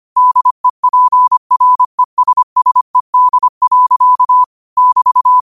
Morse code, heard as a keyed tone, about 25 wpm, 1000 Hz. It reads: DEPRESSED. <BT>